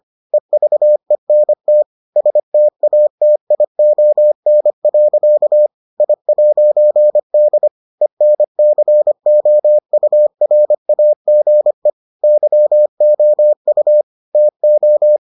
Morse code, heard as a keyed tone, about 25 wpm, 600 Hz. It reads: EVENT STATION. I'D ENCOURAGE YOU TO